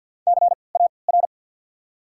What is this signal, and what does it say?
Morse code, heard as a keyed tone, about 54 wpm, spaced as for 29 wpm, 700 Hz. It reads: CAR
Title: 3x Character Spacing - Top 300 Words - 50wpm